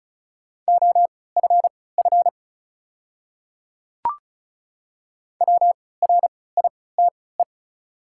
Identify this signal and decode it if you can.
Morse code, heard as a keyed tone, about 36 wpm, spaced as for 20 wpm, 700 Hz. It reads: OFF WRITE
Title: 3x Character Spacing - Top 200 Words - 35wpm